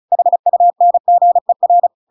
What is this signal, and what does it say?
Morse code, heard as a keyed tone, about 34 wpm, 700 Hz. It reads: HUNGER
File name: Pinocchio / Pinocchio ch8 35 35 ik1hge